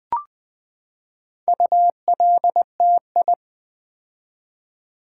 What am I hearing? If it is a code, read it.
Morse code, heard as a keyed tone, about 20 wpm, 700 Hz. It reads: ULTI
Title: Build A Word - 4 Letters At A Time - 8-Letter Words - 20wpm